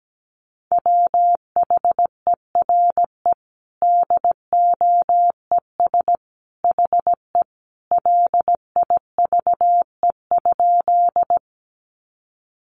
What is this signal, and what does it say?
Morse code, heard as a keyed tone, about 17 wpm, 700 Hz. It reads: WHERE DOES HE LIVE?
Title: Sentences from Top 200 Words 17wpm